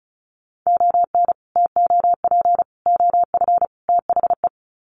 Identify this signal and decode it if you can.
Morse code, heard as a keyed tone, about 34 wpm, 700 Hz. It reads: ON TOP OF THE
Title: Sets of 4 Words 35wpm